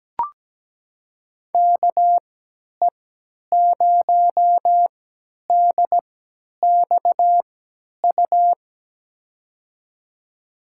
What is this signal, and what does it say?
Morse code, heard as a keyed tone, about 17 wpm, spaced as for 10 wpm, 700 Hz. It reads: KE0DXU
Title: USA Call Signs - 6 Characters - 3x Character Spacing - 17wpm